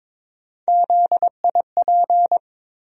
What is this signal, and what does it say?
Morse code, heard as a keyed tone, about 21 wpm, 700 Hz. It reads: ZIP
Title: USA Call Sign Postfixes - 3 Letters 22wpm